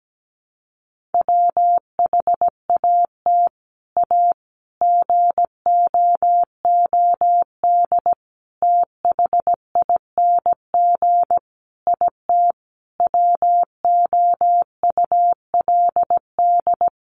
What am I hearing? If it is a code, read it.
Morse code, heard as a keyed tone, about 17 wpm, 700 Hz. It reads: WHAT A GOOD THING IT WOULD
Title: Sentences from Top 200 Words 17wpm